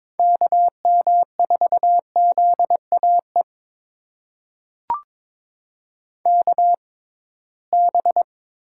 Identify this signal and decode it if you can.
Morse code, heard as a keyed tone, about 22 wpm, spaced as for 17 wpm, 700 Hz. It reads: KM4ZAE KB
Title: USA Call Signs - 6 Characters - 6x Character Spacing - 22wpm